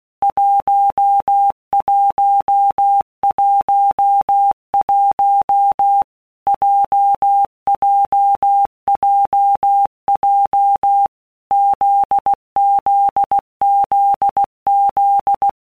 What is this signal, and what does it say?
Morse code, heard as a keyed tone, about 16 wpm, 800 Hz. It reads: 1111 JJJJ ZZZZ
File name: Basic Letter Groups By Blocks / Groups@15 / JZ17@15